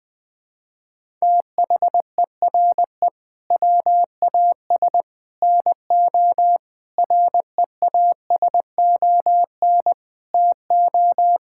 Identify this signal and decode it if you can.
Morse code, heard as a keyed tone, about 20 wpm, 700 Hz. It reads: THERE WAS NO REASON TO
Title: Sentences with 3-Word Phrases - 20wpm